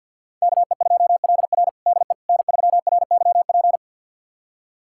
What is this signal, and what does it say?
Morse code, heard as a keyed tone, about 52 wpm, 700 Hz. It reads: KE2LR DE N3RYP
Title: USA Callsign DE Callsign - 5 Characters - 50wpm